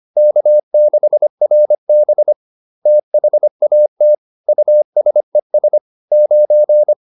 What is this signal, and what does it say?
Morse code, heard as a keyed tone, about 25 wpm, 600 Hz. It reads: K6RB THAT USES 9
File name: SS220 25